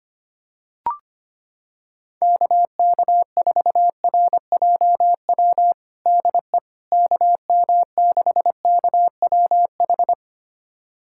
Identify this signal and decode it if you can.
Morse code, heard as a keyed tone, about 25 wpm, 700 Hz. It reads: KK4RJW DE KM6KWH
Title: USA Callsign DE Callsign 25wpm